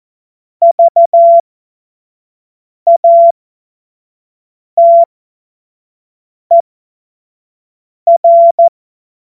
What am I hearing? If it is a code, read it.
Morse code, heard as a keyed tone, about 14 wpm, spaced as for 5 wpm, 680 Hz. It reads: VATER
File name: Morsekurs / Witze / Witz 23. August 2022 25 F70